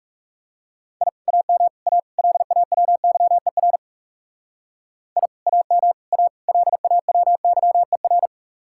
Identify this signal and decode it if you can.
Morse code, heard as a keyed tone, about 38 wpm, 700 Hz. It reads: I AM A LAWYER I AM A LAWYER